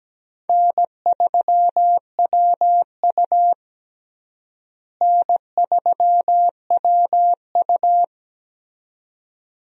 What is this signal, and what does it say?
Morse code, heard as a keyed tone, about 17 wpm, 700 Hz. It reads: N3WU N3WU
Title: USA Call Signs - 4 Characters - Repeated 2x - 17wpm